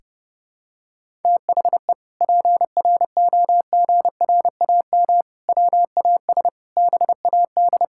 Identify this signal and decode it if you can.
Morse code, heard as a keyed tone, about 30 wpm, 700 Hz. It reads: THE PROGRAM WAS BAD
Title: Sentences from Top 1000 Words - Review Part 5 - 30wpm